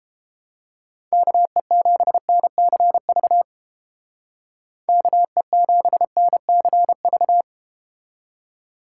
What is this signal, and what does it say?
Morse code, heard as a keyed tone, about 30 wpm, 700 Hz. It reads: KE7NCV KE7NCV